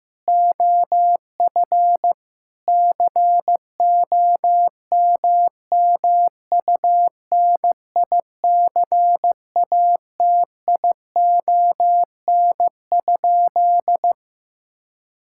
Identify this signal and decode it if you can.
Morse code, heard as a keyed tone, about 15 wpm, 700 Hz. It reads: OF COMMUNICATION?
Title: Sentences from Top 1000 Words - Review Part 6 - 15wpm